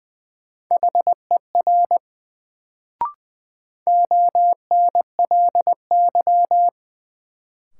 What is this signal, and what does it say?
Morse code, heard as a keyed tone, about 20 wpm, 700 Hz. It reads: HER ONLY
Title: Sets of 2 Words - New Word Drill - 20wpm